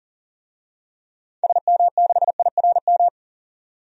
Morse code, heard as a keyed tone, about 40 wpm, 700 Hz. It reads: SM6IRM
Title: DXCC-100 Call Signs - 6 Characters - 40wpm